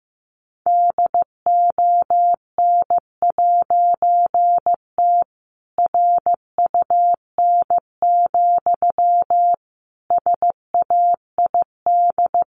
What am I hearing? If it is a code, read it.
Morse code, heard as a keyed tone, about 15 wpm, 700 Hz. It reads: DON'T RUN, SAID
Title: A Sound Of Thunder 15wpm